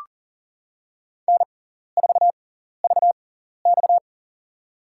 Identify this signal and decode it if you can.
Morse code, heard as a keyed tone, about 40 wpm, spaced as for 14 wpm, 700 Hz. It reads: N4VX
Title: USA Call Signs - 4 Characters - 6x Character Spacing - 40wpm